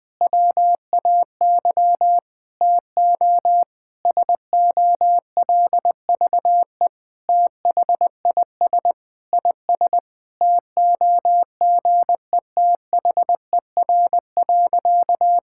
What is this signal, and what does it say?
Morse code, heard as a keyed tone, about 20 wpm, 700 Hz. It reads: WAY TO SOLVE THIS IS TOGETHER.
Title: Sentences with 4-Word Phrases - 20wpm